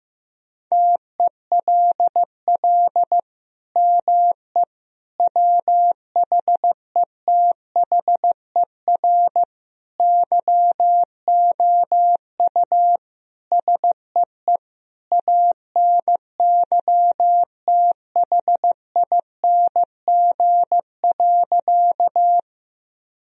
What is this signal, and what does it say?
Morse code, heard as a keyed tone, about 15 wpm, 700 Hz. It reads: TELL ME WHETHER YOU SEE ANYTHING.
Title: Sentences from Top 1000 Words - Review Part 3 - 15wpm